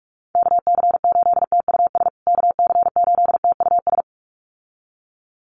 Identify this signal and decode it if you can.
Morse code, heard as a keyed tone, about 44 wpm, 700 Hz. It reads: KC8TUS KC8TUS
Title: USA Call Signs Repeated 2x 45wpm